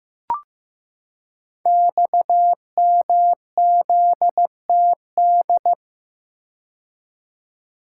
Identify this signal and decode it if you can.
Morse code, heard as a keyed tone, about 15 wpm, 700 Hz. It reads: XMZTD
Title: Code Groups - 5 Letter:Numbers - 15wpm